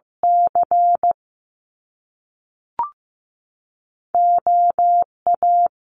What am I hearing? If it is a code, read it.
Morse code, heard as a keyed tone, about 15 wpm, 700 Hz. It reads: C O A
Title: Top Two Letter Combinations in Words 15wpm